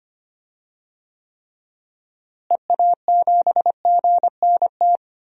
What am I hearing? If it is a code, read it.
Morse code, heard as a keyed tone, about 25 wpm, 700 Hz. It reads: EA7GNT
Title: DXCC-100 Call Signs - 6 Characters - Repeated 3x - 25wpm